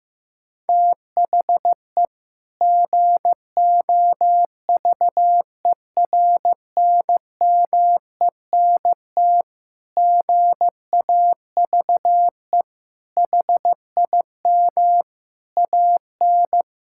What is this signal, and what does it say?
Morse code, heard as a keyed tone, about 15 wpm, 700 Hz. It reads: THE GOVERNMENT GAVE HIM AN
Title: Sentences from Top 1000 Words - Review Part 5 - 15wpm